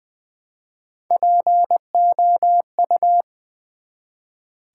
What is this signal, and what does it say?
Morse code, heard as a keyed tone, about 20 wpm, 700 Hz. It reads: POU
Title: Build A Word - 1 Letter At A Time - 3 to 5-Letter Words - 20wpm